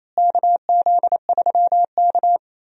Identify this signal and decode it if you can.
Morse code, heard as a keyed tone, about 28 wpm, 700 Hz. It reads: KZ3K
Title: USA Call Signs - 4 Characters - Repeated 3x - 28wpm